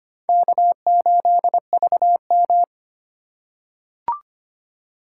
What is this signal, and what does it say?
Morse code, heard as a keyed tone, about 25 wpm, 700 Hz. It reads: K8VM
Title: USA Call Signs - 2x Character Spacing - 25wpm